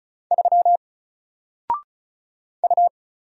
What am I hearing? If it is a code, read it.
Morse code, heard as a keyed tone, about 34 wpm, 700 Hz. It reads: 3 U
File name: Single-Letter-Number-35wpm